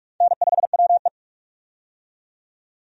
Morse code, heard as a keyed tone, about 46 wpm, 700 Hz. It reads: N5WE
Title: USA Call Signs - 4 Characters - 45wpm